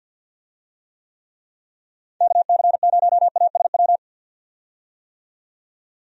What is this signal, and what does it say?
Morse code, heard as a keyed tone, about 51 wpm, 700 Hz. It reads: KC0ASW
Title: USA Call Signs - 6 Characters - 50wpm